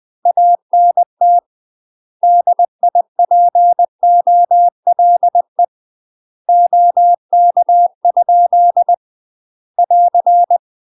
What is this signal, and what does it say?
Morse code, heard as a keyed tone, about 20 wpm, 700 Hz. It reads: ANT DIPOLE OK? <AR>
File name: UK MASTER QSOs TRAINING / 20WPM / UK MASTER QSOs 0045